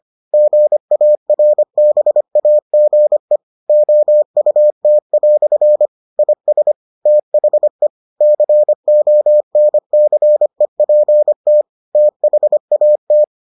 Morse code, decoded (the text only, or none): GARBAGE OUT" IS THE CONCEPT THAT